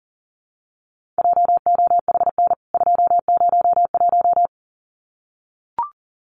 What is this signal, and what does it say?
Morse code, heard as a keyed tone, about 40 wpm, 700 Hz. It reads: JOHN 201
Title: CWT Exchange 40wpm